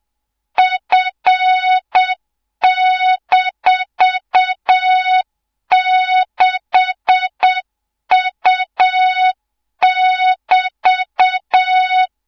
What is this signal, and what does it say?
Morse code, heard as a keyed tone, about 7 wpm, 3700 Hz, with noise. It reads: F-6U<BT>
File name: Morze-HA2MN / hang / 24 GYAKORLAT